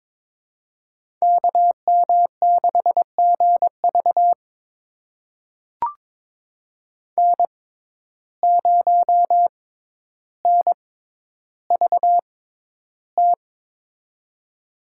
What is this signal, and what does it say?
Morse code, heard as a keyed tone, about 22 wpm, spaced as for 11 wpm, 700 Hz. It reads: KM6GV N0NVT
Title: USA Call Signs - 5 Characters - 6x Character Spacing - 22wpm